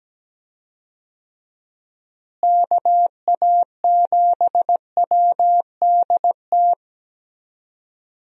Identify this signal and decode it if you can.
Morse code, heard as a keyed tone, about 17 wpm, 700 Hz. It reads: KA7WDT
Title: USA Call Signs - 6 Characters - Repeated 3x - 17wpm